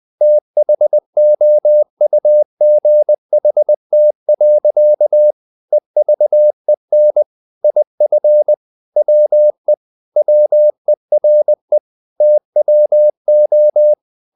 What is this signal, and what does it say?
Morse code, heard as a keyed tone, about 20 wpm, 600 Hz. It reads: THOUGHT. EVEN IF WE WERE TWO
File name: the old man and the sea part 2 20wpm